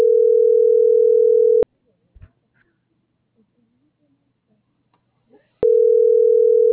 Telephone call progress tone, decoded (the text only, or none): ringback tone